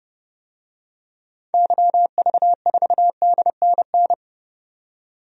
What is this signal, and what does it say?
Morse code, heard as a keyed tone, about 31 wpm, 700 Hz. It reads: YV4DNN
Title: DXCC-100 Call Signs - 6 Characters - 30wpm